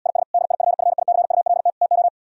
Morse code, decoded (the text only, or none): IS DIFFICULT TO